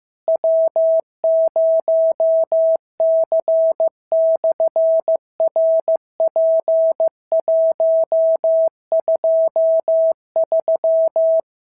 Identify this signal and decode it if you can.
Morse code, heard as a keyed tone, about 15 wpm, 650 Hz. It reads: W0C/RP123
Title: SOTA QSO Practice 15wpm